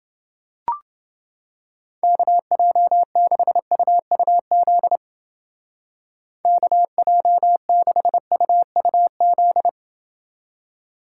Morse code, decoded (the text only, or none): KJ6UUZ KJ6UUZ